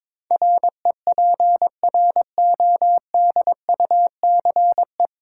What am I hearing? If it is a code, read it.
Morse code, heard as a keyed tone, about 22 wpm, 700 Hz. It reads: REPRODUCE
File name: 9 Letter Words - First 3 Letters Given - 22wpm